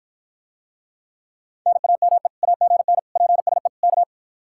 Morse code, heard as a keyed tone, about 54 wpm, 700 Hz. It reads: NAME AGN PSE K